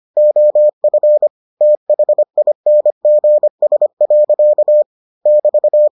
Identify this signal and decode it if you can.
Morse code, heard as a keyed tone, about 25 wpm, 600 Hz. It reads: OF THINGS. <BT>